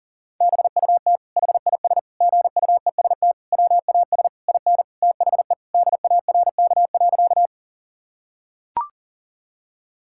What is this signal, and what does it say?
Morse code, heard as a keyed tone, about 40 wpm, 700 Hz. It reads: BUT HIS GUEST WAS IN THE DARK.
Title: Sentences from Top 1600 Words 40wpm